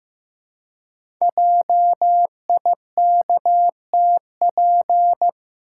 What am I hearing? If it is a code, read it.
Morse code, heard as a keyed tone, about 15 wpm, 700 Hz. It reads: JIKTP
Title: Code Groups - 5 Letters - 15wpm